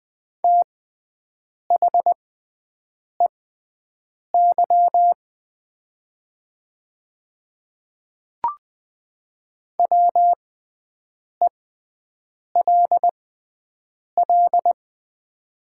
Morse code, heard as a keyed tone, about 20 wpm, spaced as for 7 wpm, 700 Hz. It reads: THEY WELL
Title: 6x Character Spacing - Top 200 Words - 20wpm